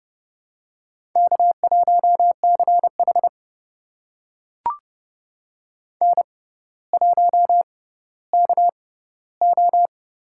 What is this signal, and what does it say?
Morse code, heard as a keyed tone, about 30 wpm, spaced as for 15 wpm, 700 Hz. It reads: K1CH N1KO
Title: USA Call Signs - 6x Character Spacing - 30wpm